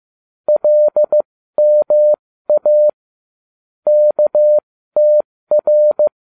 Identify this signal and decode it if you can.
Morse code, heard as a keyed tone, about 15 wpm, spaced as for 13 wpm, 600 Hz. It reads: LMA KTR